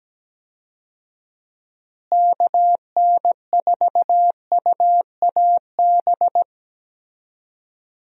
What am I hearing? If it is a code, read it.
Morse code, heard as a keyed tone, about 17 wpm, 700 Hz. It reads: KN4UAB
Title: USA Call Signs - 6 Characters - Repeated 2x - 17wpm